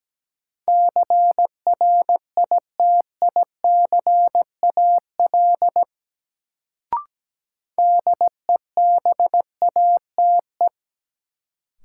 Morse code, heard as a keyed tone, about 17 wpm, 700 Hz. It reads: CRITICAL DEBATE
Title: Sentences from Top 1100 Words - New Word Drill Alt - 17wpm